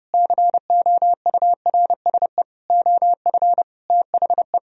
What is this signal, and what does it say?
Morse code, heard as a keyed tone, about 32 wpm, 700 Hz. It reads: COURSE OF THE